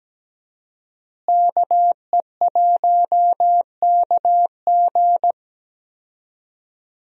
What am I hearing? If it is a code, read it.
Morse code, heard as a keyed tone, about 17 wpm, 700 Hz. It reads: KE1KG
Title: USA Call Signs - 5 Characters - Repeated 2x - 17wpm